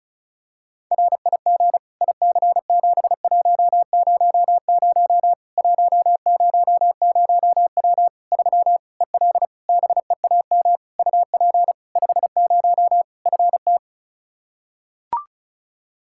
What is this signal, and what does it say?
Morse code, heard as a keyed tone, about 34 wpm, 700 Hz. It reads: RIG IC7100 100W 3 EL BEAM UP 50 FT
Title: Practice QSO - Exchange 2 - Info - 35wpm